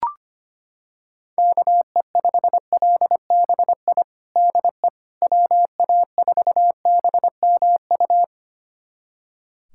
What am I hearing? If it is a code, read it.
Morse code, heard as a keyed tone, about 25 wpm, 700 Hz. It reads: KE5LBI DE WA4BMU
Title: USA Callsign DE Callsign - 6 Characters - 25wpm